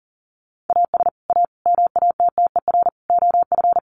Morse code, heard as a keyed tone, about 40 wpm, 700 Hz. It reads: AS A MATTER OF
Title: Sets of 4 Words 40wpm